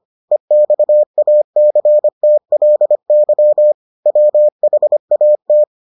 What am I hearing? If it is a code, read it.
Morse code, heard as a keyed tone, about 25 wpm, 600 Hz. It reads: EXACTLY WHAT